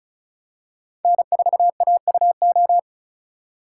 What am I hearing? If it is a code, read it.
Morse code, heard as a keyed tone, about 36 wpm, 700 Hz. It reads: N4AUO
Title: USA Call Signs - 4x Character Spacing - 35wpm